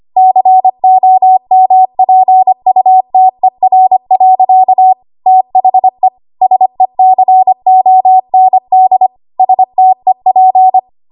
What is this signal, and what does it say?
Morse code, heard as a keyed tone, about 25 wpm, 750 Hz. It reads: COMPUTER. THE SECOND STEP
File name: Archive / 25 WPM / 25WPM